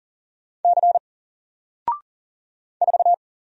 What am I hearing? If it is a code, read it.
Morse code, heard as a keyed tone, about 39 wpm, 700 Hz. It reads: C 4
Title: Single Letter-Number Alt 40wpm